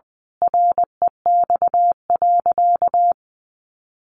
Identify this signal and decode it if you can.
Morse code, heard as a keyed tone, about 20 wpm, 700 Hz. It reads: REX.
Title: A Sound Of Thunder 20wpm